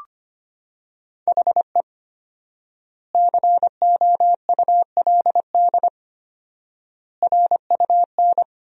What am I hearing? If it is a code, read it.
Morse code, heard as a keyed tone, about 25 wpm, 700 Hz. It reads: HE COULD RUN
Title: 4x Word-Spacing - Sentences from Top 300 Words - 25wpm